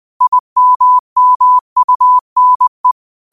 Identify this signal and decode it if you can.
Morse code, heard as a keyed tone, about 20 wpm, 1000 Hz. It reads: IMMUNE
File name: qotdfiles / 20wpm